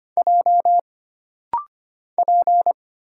Morse code, heard as a keyed tone, about 25 wpm, 700 Hz. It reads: J P
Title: Single Letter-Number - Rapid-Fire 25wpm